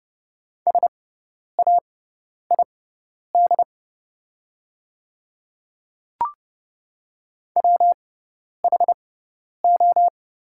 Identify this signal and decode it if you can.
Morse code, heard as a keyed tone, about 30 wpm, spaced as for 10 wpm, 690 Hz. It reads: SAID WHO
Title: 6x Character Spacing - Top 100 Words - 30wpm